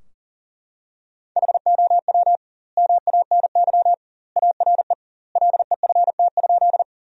Morse code, heard as a keyed tone, about 40 wpm, 700 Hz. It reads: HOW MANY ARE LEFT?
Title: 2x Word-Spacing - Sentences from Top 300 Words - 40wpm